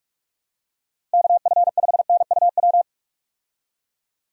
Morse code, heard as a keyed tone, about 42 wpm, 700 Hz. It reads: KF5NUW
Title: USA Call Signs - 6 Characters - Repeated 2x - 45wpm